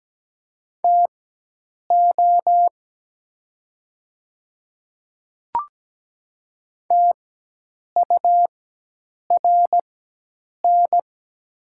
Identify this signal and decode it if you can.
Morse code, heard as a keyed tone, about 17 wpm, spaced as for 8 wpm, 700 Hz. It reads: TO TURN
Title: 4x Character Spacing - Top 200 Words - 17wpm